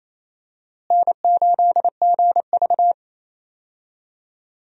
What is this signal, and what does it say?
Morse code, heard as a keyed tone, about 28 wpm, 700 Hz. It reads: N8GV
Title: USA Call Signs - 4 Characters - Repeated 2x - 28wpm